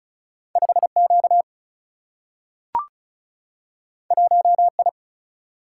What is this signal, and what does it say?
Morse code, heard as a keyed tone, about 36 wpm, 700 Hz. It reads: 5Q 1I
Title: Single-Letter Single-Number 35wpm